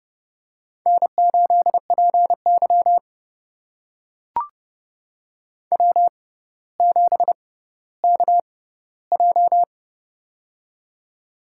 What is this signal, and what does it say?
Morse code, heard as a keyed tone, about 30 wpm, spaced as for 15 wpm, 700 Hz. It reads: N8PY W7KJ
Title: USA Call Signs - 4 Characters - 6x Character Spacing - 30wpm